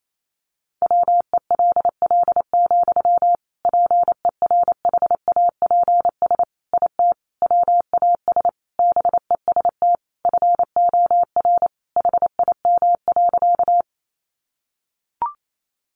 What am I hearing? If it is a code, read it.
Morse code, heard as a keyed tone, about 28 wpm, 700 Hz. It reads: WELL, PERHAPS IT WAS BEST FOR HIM.
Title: Sentences from Top 500 Words 28wpm